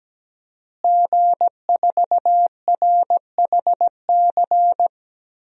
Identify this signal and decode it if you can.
Morse code, heard as a keyed tone, about 17 wpm, 700 Hz. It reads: G4RHC
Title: DXCC-100 Call Signs - 5 Characters - Repeated 3x - 17wpm